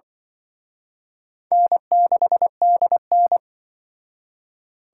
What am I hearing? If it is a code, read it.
Morse code, heard as a keyed tone, about 25 wpm, 700 Hz. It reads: N6DN
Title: Speed Racing - USA 1x2 Call Signs 20wpm